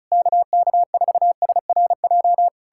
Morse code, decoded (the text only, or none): KK4SRJ